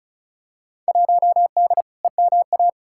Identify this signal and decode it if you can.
Morse code, heard as a keyed tone, about 34 wpm, 700 Hz. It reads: 1D EMA
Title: Field Day Exchanges - 35wpm